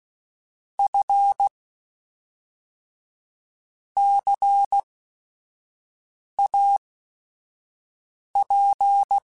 Morse code, heard as a keyed tone, about 16 wpm, spaced as for 5 wpm, 800 Hz. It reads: F CAP
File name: Technical Subjects / Technical Subjects-mp3s / XC @5